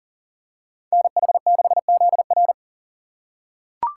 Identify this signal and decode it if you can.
Morse code, heard as a keyed tone, about 37 wpm, 700 Hz. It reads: NH6ZR
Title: USA Call Signs - 5 Characters - 4x Character Spacing - 40wpm